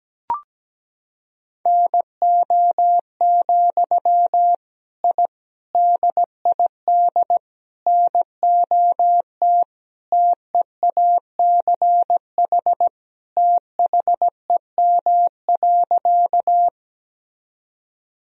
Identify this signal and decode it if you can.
Morse code, heard as a keyed tone, about 17 wpm, 700 Hz. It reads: NO, I DID NOT TEACH THEM.
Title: Sentences from Top 1000 Words - Review Part 2 - 17wpm